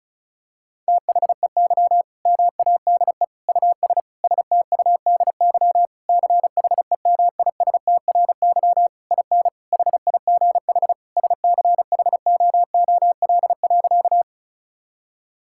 Morse code, heard as a keyed tone, about 36 wpm, 700 Hz. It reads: THEY MADE US STUDY CHEMISTRY IN HIGH SCHOOL.
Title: Sentences from Top 1000 Words - Review Part 6 - 35wpm